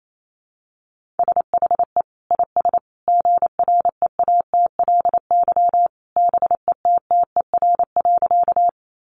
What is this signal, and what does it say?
Morse code, heard as a keyed tone, about 28 wpm, 700 Hz. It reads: SHE IS GREATLY BETTER.
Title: Sentences from Top 1400 Words 28wpm